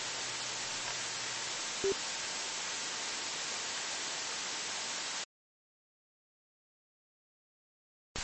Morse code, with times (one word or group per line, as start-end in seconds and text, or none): none